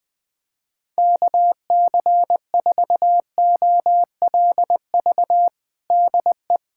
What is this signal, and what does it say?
Morse code, heard as a keyed tone, about 20 wpm, 700 Hz. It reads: KC4OLV DE
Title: USA Callsign DE Callsign - 6 Characters - 20wpm